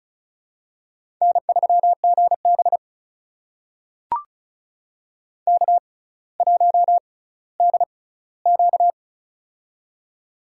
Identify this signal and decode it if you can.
Morse code, heard as a keyed tone, about 36 wpm, spaced as for 18 wpm, 700 Hz. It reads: N3GB K1DQ
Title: USA Call Signs - 4 Characters - 6x Character Spacing - 35wpm